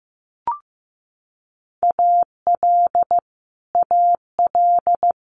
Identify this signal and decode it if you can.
Morse code, heard as a keyed tone, about 15 wpm, 700 Hz. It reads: AL AL
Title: Multi-letter Phonemes Repeated 3x 15wpm